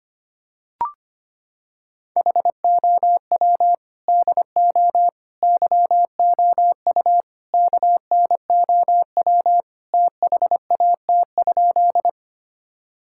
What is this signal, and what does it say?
Morse code, heard as a keyed tone, about 25 wpm, 700 Hz. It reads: HOW DO YOU KNOW THAT?
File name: practice / 25wpm / Sentences with 4-Word Phrases - 25wpm